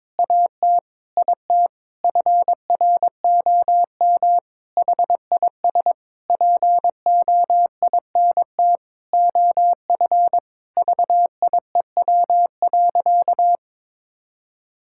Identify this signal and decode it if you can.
Morse code, heard as a keyed tone, about 21 wpm, 700 Hz. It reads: AT IT FROM HIS POINT OF VIEW.
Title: Sentences from Top 1000 Words - Review Part 6 - 22wpm